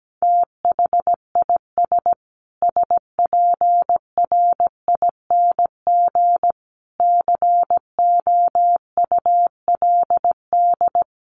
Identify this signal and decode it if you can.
Morse code, heard as a keyed tone, about 17 wpm, 700 Hz. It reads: THIS SPRING COULD
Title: Sentences from Top 1500 Words 17wpm